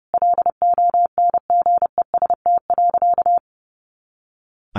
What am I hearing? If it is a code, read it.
Morse code, heard as a keyed tone, about 30 wpm, 700 Hz. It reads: LONGEST.
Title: A Sound Of Thunder 30wpm